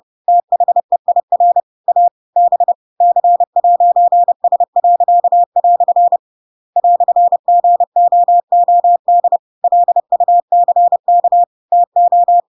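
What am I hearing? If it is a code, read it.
Morse code, heard as a keyed tone, about 30 wpm, 700 Hz. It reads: THEIR A B C'S." "GOOD LUCK TO
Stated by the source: Chapter 2 of The Adventures of Pinocchio read as a book in Morse code (CW) at 30 wpm, by ik1hge.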